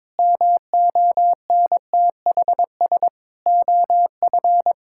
Morse code, heard as a keyed tone, about 22 wpm, 700 Hz. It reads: MONTHS OF
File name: Sentences with 7-Word Phrases - 22wpm